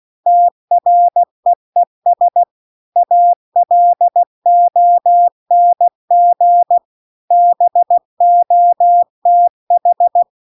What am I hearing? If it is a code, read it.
Morse code, heard as a keyed tone, about 16 wpm, 700 Hz. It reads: TREES ALONG BOTH
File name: CW-The-Big-Twoheaded-River-2-16WPM